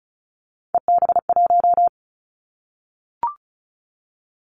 Morse code, heard as a keyed tone, about 34 wpm, 700 Hz. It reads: EB1
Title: Callsign Prefixes for Top 25 DXCC - Repeated 3x - 35wpm